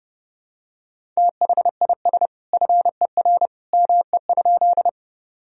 Morse code, heard as a keyed tone, about 30 wpm, 700 Hz. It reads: THIS FER ME?
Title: Sentences from Top 200 Words and Abbreviations 30wpm